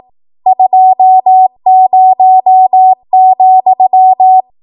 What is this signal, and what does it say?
Morse code, heard as a keyed tone, about 18 wpm, 750 Hz. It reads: 20,
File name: Archive / 18 WPM / 18WPM